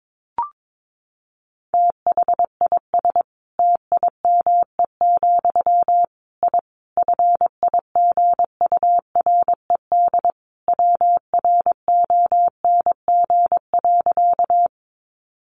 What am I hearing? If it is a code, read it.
Morse code, heard as a keyed tone, about 22 wpm, 700 Hz. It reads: THIS TIME, I FIGURED WRONG.
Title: A Sound Of Thunder 22wpm